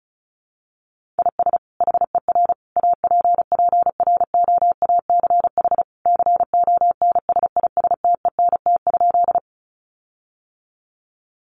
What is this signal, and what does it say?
Morse code, heard as a keyed tone, about 34 wpm, 700 Hz. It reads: IS HER APPROACH CONSISTENT?
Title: Sentences from Top 1500 Words 35wpm